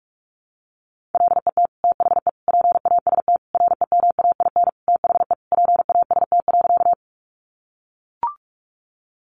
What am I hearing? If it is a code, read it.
Morse code, heard as a keyed tone, about 44 wpm, 700 Hz. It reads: LET THE PAST REMAIN THE PAST.
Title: Sentences from Top 1200 Words 45wpm